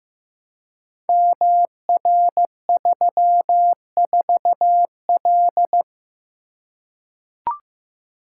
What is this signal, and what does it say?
Morse code, heard as a keyed tone, about 15 wpm, 700 Hz. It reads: MR34L
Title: Code Groups - 5 Letter:Numbers - 15wpm